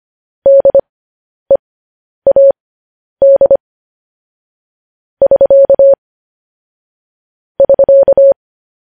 Morse code, heard as a keyed tone, about 25 wpm, spaced as for 10 wpm, 550 Hz. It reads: DEAD <SK> <SK>
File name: podcast / News headlines-1x-25x10